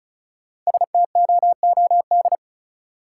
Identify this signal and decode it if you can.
Morse code, heard as a keyed tone, about 36 wpm, 700 Hz. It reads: STOOD
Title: Sentences from Top 500 Words - New Word Drill - 35wpm